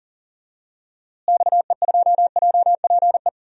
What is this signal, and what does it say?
Morse code, heard as a keyed tone, about 40 wpm, 700 Hz. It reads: XE2JPE
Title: DXCC-100 Call Signs - 6 Characters - Repeated 2x - 40wpm